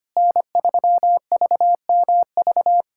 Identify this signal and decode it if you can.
Morse code, heard as a keyed tone, about 25 wpm, 700 Hz. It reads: N3VMV
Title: USA Callsign DE Callsign - 5 Characters - 25wpm